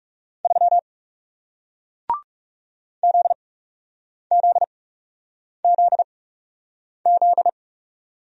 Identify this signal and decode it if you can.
Morse code, heard as a keyed tone, about 40 wpm, spaced as for 8 wpm, 700 Hz. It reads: 3 ZZZZ